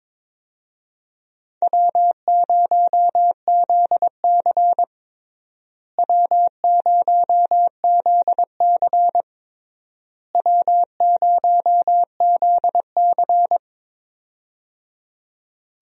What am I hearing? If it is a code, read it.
Morse code, heard as a keyed tone, about 21 wpm, 700 Hz. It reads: W0ZC W0ZC W0ZC
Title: USA Call Signs - 4 Characters - Repeated 3x - 22wpm